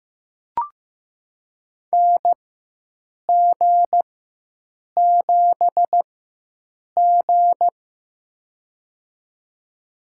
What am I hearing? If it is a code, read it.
Morse code, heard as a keyed tone, about 15 wpm, spaced as for 7 wpm, 700 Hz. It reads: NG7G
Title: USA Call Signs - 4x Character Spacing - 15wpm